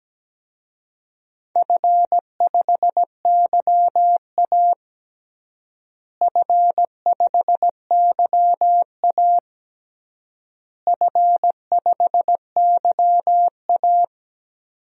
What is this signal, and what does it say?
Morse code, heard as a keyed tone, about 17 wpm, 700 Hz. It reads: F5YA F5YA F5YA
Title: DXCC-100 Call Signs - 4 Characters - Repeated 3x - 17wpm